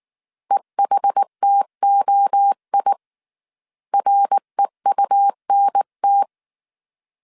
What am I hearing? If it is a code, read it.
Morse code, heard as a keyed tone, about 19 wpm, 800 Hz. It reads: EHTOI REUNT